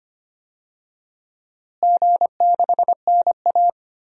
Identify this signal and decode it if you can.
Morse code, heard as a keyed tone, about 25 wpm, 700 Hz. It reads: G6NA
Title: DXCC-100 Call Signs - 4 Characters - Repeated 2x - 25wpm